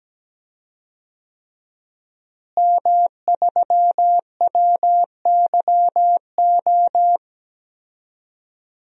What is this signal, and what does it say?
Morse code, heard as a keyed tone, about 17 wpm, 700 Hz. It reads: M3WYO
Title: DXCC-100 Call Signs - 5 Characters - Repeated 3x - 17wpm